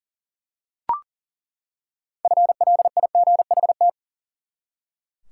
Morse code, heard as a keyed tone, about 40 wpm, 700 Hz. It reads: FLIGHT
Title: Six Letter Words - 40wpm